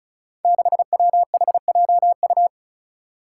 Morse code, heard as a keyed tone, about 36 wpm, 700 Hz. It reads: 6WHJU
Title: Five Challenging Letter-Number Combinations 35wpm